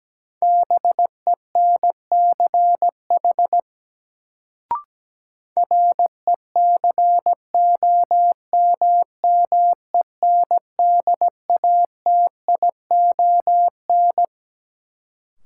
Morse code, heard as a keyed tone, about 17 wpm, 700 Hz. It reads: BENCH RECOMMENDATION
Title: Sentences from Top 1400 Words - New Word Drill - 17wpm